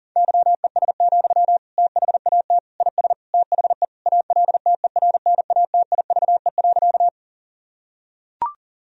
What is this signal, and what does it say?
Morse code, heard as a keyed tone, about 40 wpm, 700 Hz. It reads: YES, THAT IS THE ALTERNATIVE.
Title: Sentences from Top 1000 Words - Review Part 4 - 40wpm